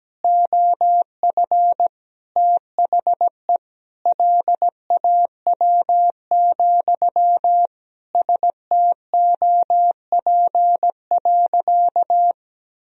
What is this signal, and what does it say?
Morse code, heard as a keyed tone, about 17 wpm, 700 Hz. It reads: OF THE LAW, STOP.
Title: Sentences from Top 1000 Words - Review Part 3 - 17wpm